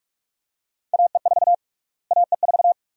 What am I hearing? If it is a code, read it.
Morse code, heard as a keyed tone, about 47 wpm, 700 Hz. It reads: AE4 AE4
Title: USA Call Sign Prefixes - Repeated 2x - 45wpm